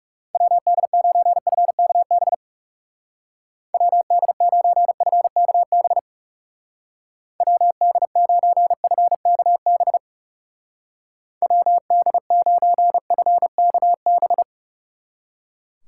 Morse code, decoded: WD9FKB WD9FKB WD9FKB WD9FKB

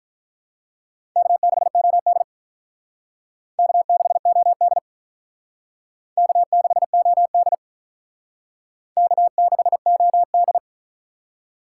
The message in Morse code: K6OD K6OD K6OD K6OD

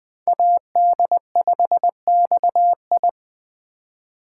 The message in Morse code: AD5XI

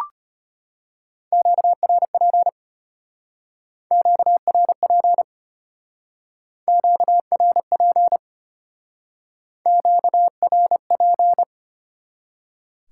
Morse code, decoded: QRP QRP QRP QRP

